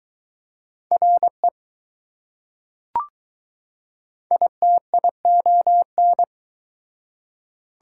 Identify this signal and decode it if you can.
Morse code, heard as a keyed tone, about 20 wpm, 700 Hz. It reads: RE ITION